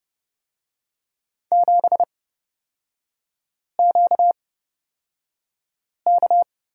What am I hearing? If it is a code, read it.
Morse code, heard as a keyed tone, about 31 wpm, spaced as for 5 wpm, 690 Hz. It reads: 7QK
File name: course / 30:10wpm / 236 Lesson 38e - Letter q - Three Characters 30wpm